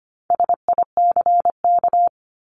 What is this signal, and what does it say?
Morse code, heard as a keyed tone, about 25 wpm, 700 Hz. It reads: SICK
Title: Top 1501-1750 Words 25wpm